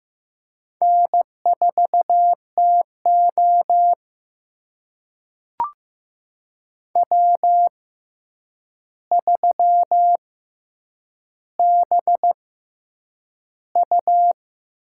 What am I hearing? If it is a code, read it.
Morse code, heard as a keyed tone, about 15 wpm, spaced as for 8 wpm, 700 Hz. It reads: N4TO W3BU
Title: USA Call Signs - 4 Characters - 6x Character Spacing - 15wpm